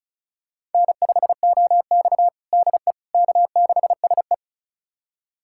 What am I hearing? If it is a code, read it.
Morse code, heard as a keyed tone, about 34 wpm, 700 Hz. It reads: N5OX DE K6SE
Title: USA Callsign DE Callsign - 4 Characters - 35wpm